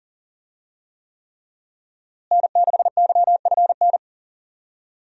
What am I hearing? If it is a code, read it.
Morse code, heard as a keyed tone, about 40 wpm, 700 Hz. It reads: N6YFN
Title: USA Call Signs - 5 Characters - Repeated 3x - 40wpm